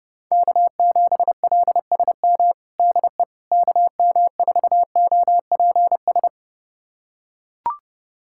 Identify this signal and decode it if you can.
Morse code, heard as a keyed tone, about 30 wpm, 700 Hz. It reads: K7LSM DE KM4OPS
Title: USA Callsign DE Callsign 30wpm